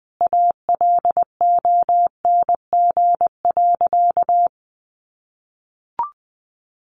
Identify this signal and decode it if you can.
Morse code, heard as a keyed tone, about 20 wpm, 700 Hz. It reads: ALONG.
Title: Sentences from Top 1500 Words 20wpm